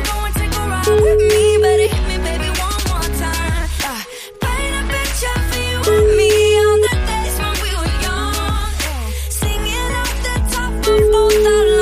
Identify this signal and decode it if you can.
Ringback tone (Europe, 425 Hz), with noise.